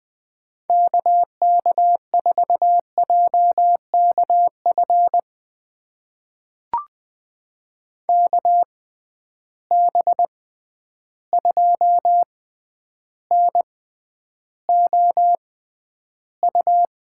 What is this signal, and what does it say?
Morse code, heard as a keyed tone, about 20 wpm, spaced as for 10 wpm, 700 Hz. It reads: KK4JKF KB2NOU
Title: USA Call Signs - 6 Characters - 6x Character Spacing - 20wpm